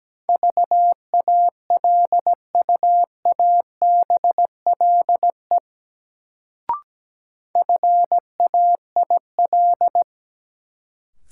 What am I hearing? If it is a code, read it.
Morse code, heard as a keyed tone, about 17 wpm, 700 Hz. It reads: VALUABLE FAIL
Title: Sentences from Top 1100 Words - New Word Drill - 17wpm